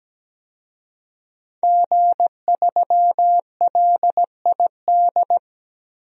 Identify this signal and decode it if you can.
Morse code, heard as a keyed tone, about 17 wpm, 700 Hz. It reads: G3LID
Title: DXCC-100 Call Signs - 5 Characters - 17wpm